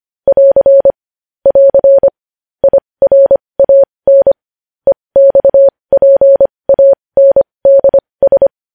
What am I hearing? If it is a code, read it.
Morse code, heard as a keyed tone, about 25 wpm, spaced as for 20 wpm, 550 Hz. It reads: <AR> <AR> IRAN EXPANDS
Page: news headlines in morse code at 20 wpm